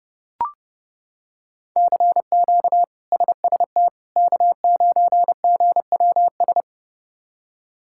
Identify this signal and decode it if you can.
Morse code, heard as a keyed tone, about 30 wpm, 700 Hz. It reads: CQ SST K9GWS
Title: K1USN SST QSO Practice - 30wpm